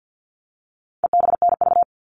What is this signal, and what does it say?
Morse code, heard as a keyed tone, about 50 wpm, 700 Hz. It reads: EBNV
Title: All Cut Numbers - 4-Digit 50wpm